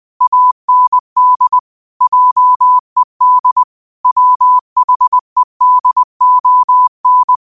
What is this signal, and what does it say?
Morse code, heard as a keyed tone, about 20 wpm, 1000 Hz. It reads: AND JED WHEDON